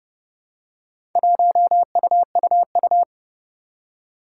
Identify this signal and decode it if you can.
Morse code, heard as a keyed tone, about 30 wpm, 700 Hz. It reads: 1UUU
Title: Four Challenging Letter-Number Combinations 30wpm